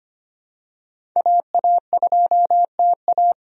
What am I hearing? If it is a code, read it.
Morse code, heard as a keyed tone, about 25 wpm, 700 Hz. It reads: AA2TA
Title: USA Call Signs - 5 Characters - 25wpm